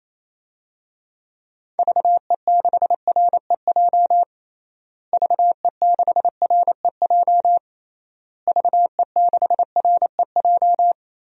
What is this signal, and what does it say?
Morse code, heard as a keyed tone, about 28 wpm, 700 Hz. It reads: VE6REJ VE6REJ VE6REJ
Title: DXCC-100 Call Signs - 6 Characters - Repeated 3x - 28wpm